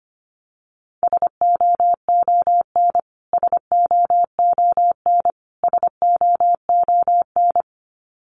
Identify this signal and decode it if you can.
Morse code, heard as a keyed tone, about 25 wpm, 700 Hz. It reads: SOON SOON SOON
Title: Top 500 Words Repeated 25wpm